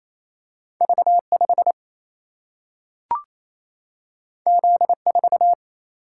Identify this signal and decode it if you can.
Morse code, heard as a keyed tone, about 28 wpm, 700 Hz. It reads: V5 Z4
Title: Single-Letter Single-Number 28wpm